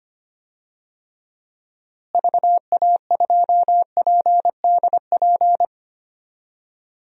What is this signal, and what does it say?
Morse code, heard as a keyed tone, about 25 wpm, 700 Hz. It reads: VA2PDP
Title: DXCC-100 Call Signs - 6 Characters - Repeated 2x - 25wpm